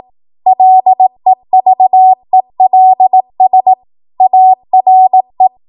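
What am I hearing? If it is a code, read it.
Morse code, heard as a keyed tone, about 18 wpm, 750 Hz. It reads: LEVELS ARE
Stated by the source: Code practice at 18 WPM, a transition run bridging 15 to 20 WPM.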